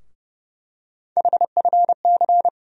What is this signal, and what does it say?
Morse code, heard as a keyed tone, about 30 wpm, 700 Hz. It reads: HFC
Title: USA Call Sign Postfixes - 3 Letters 30wpm